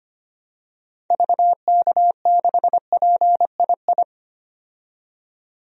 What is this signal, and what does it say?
Morse code, heard as a keyed tone, about 25 wpm, 700 Hz. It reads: VK6PII
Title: DXCC-100 Call Signs - 6 Characters - 25wpm